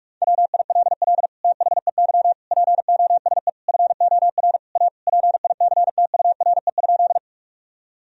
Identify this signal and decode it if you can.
Morse code, heard as a keyed tone, about 41 wpm, 700 Hz. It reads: WILL THEY POSE FOR A PICTURE?
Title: Sentences from Top 1000 Words - Review Part 2 - 45wpm